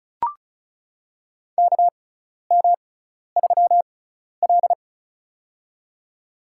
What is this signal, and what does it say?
Morse code, heard as a keyed tone, about 36 wpm, spaced as for 12 wpm, 700 Hz. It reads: KM3L
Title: USA Call Signs - 4 Characters - 6x Character Spacing - 35wpm